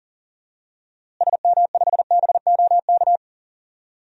Morse code, heard as a keyed tone, about 40 wpm, 700 Hz. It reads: SM5BOK
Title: DXCC-100 Call Signs - 6 Characters - Repeated 3x - 40wpm